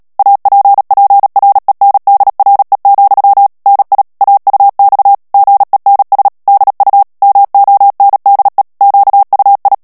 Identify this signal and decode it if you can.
Morse code, heard as a keyed tone, about 37 wpm, 800 Hz. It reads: APPRENDRE, NI AUX GENS DU MONDE QUI
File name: CW / FICHIERS-MP3 / FRANCAIS TEXTE CLAIR 2 VITESSE 35 MOTS MINUTES